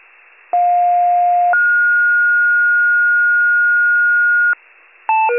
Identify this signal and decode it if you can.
Two-tone page, heard as a one-second tone, then a three-second tone, about 710 then 1430 Hz. It is heard from 0.5 to 4.5 s.